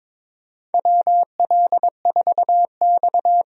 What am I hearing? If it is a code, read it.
Morse code, heard as a keyed tone, about 22 wpm, 700 Hz. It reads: WL4X
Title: USA Call Signs - 4 Characters - 22wpm